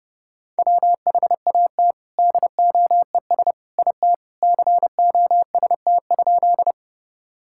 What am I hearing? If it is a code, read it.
Morse code, heard as a keyed tone, about 32 wpm, 700 Hz. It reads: WHAT DOES IT COST?
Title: Sentences from Top 1000 Words - Review Part 5 - 30wpm